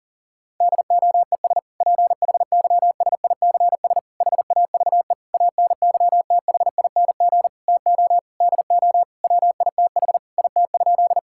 Morse code, heard as a keyed tone, about 40 wpm, 700 Hz. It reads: DOES PHYSICS HAVE ANYTHING TO DO WITH IT?
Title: Sentences from Top 1000 Words - Review Part 5 - 40wpm